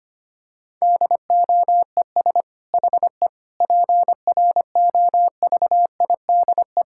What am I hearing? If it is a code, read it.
Morse code, heard as a keyed tone, about 25 wpm, 700 Hz. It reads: DOES HE PROVIDE
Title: Sentences from Top 1000 Words - Review Part 6 - 25wpm